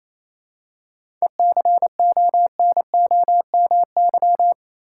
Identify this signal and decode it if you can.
Morse code, heard as a keyed tone, about 28 wpm, 700 Hz. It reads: ECONOMY
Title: Seven Letter Words - 28wpm